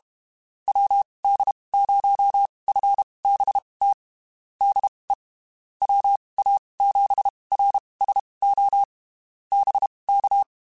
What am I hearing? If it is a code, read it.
Morse code, heard as a keyed tone, about 32 wpm, spaced as for 23 wpm, 800 Hz. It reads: WD0FBT DE WA7RSO BK
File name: (spaced) / QSOs Spaced-mp3s / WD0FBT @30 sp